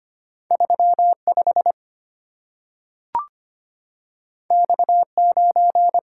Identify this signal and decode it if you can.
Morse code, heard as a keyed tone, about 25 wpm, 690 Hz. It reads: 35 X9
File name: Two Letter-Number Combinations - 25wpm